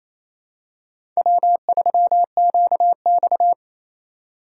Morse code, heard as a keyed tone, about 28 wpm, 700 Hz. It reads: W3QX
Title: USA Call Signs - 4x Character Spacing - 28wpm